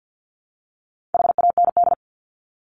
Morse code, heard as a keyed tone, about 50 wpm, 700 Hz. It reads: HAND
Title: Top 200 Words 50wpm